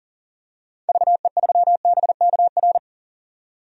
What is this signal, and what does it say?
Morse code, heard as a keyed tone, about 37 wpm, 700 Hz. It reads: VE3BKR
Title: DXCC-100 Call Signs - 6 Characters - 40wpm